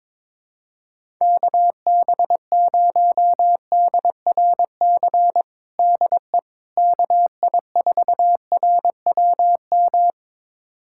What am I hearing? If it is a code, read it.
Morse code, heard as a keyed tone, about 22 wpm, 700 Hz. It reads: KB0DRC DE KI4RWM
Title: USA Callsign DE Callsign - 6 Characters - 22wpm